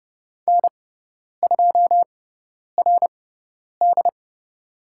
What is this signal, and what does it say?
Morse code, heard as a keyed tone, about 30 wpm, spaced as for 10 wpm, 700 Hz. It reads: N2RD
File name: Lesson 17g - Letter u - Call Signs 30wpm